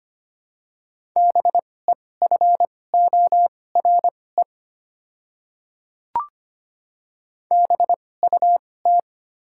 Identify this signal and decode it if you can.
Morse code, heard as a keyed tone, about 25 wpm, spaced as for 18 wpm, 700 Hz. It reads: BEFORE BUT
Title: 2x Character Spacing - Top 200 Words - 25wpm